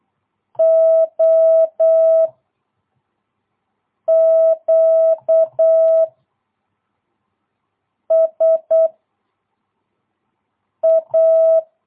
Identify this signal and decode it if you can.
Morse code, heard as a keyed tone, about 8 wpm, 650 Hz. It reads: O Q S A
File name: jh8cbh / 2022cw